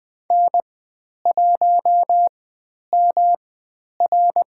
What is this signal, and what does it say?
Morse code, heard as a keyed tone, about 20 wpm, spaced as for 10 wpm, 700 Hz. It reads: N1MR
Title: 109 Lesson 18g - Letter m - Call Signs 20wpm